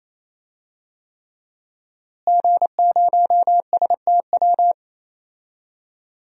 Morse code, heard as a keyed tone, about 28 wpm, 700 Hz. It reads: G0STW
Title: DXCC-100 Call Signs - 5 Characters - Repeated 3x - 28wpm